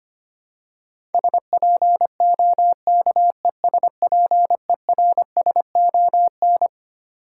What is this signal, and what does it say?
Morse code, heard as a keyed tone, about 25 wpm, 700 Hz. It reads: SPOKESPERSON
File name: Twelve Letter Words - 25wpm